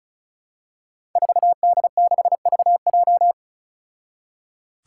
Morse code, heard as a keyed tone, about 34 wpm, 700 Hz. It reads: 4D6VJ
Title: Five Challenging Letter-Number Combinations 35wpm